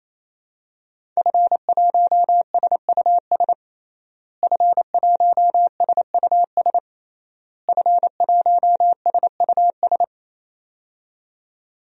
Morse code, heard as a keyed tone, about 28 wpm, 700 Hz. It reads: F1SUS F1SUS F1SUS
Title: DXCC-100 Call Signs - 5 Characters - Repeated 3x - 28wpm